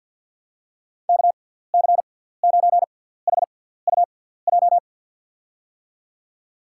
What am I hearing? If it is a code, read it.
Morse code, heard as a keyed tone, about 48 wpm, spaced as for 17 wpm, 700 Hz. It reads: KC9HUJ
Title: USA Call Signs - 6x Character Spacing - 50wpm